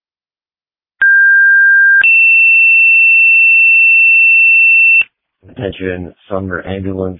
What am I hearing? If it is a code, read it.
Two-tone page, heard as a one-second tone, then a three-second tone, about 1600 then 2700 Hz, with speech. It is heard from 1.0 to 5.0 s.